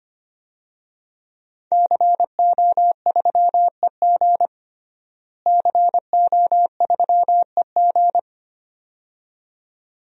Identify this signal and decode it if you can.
Morse code, heard as a keyed tone, about 25 wpm, 700 Hz. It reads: CO3EG CO3EG
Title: DXCC-100 Call Signs - 5 Characters - Repeated 2x - 25wpm